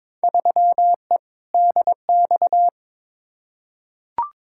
Morse code, heard as a keyed tone, about 22 wpm, 700 Hz. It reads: 3E DX
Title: Field Day Exchanges Encore - 22wpm